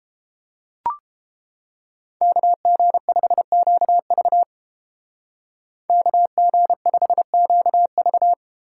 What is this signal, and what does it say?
Morse code, heard as a keyed tone, about 31 wpm, 700 Hz. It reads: KG5QV KG5QV